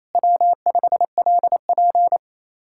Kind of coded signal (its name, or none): Morse code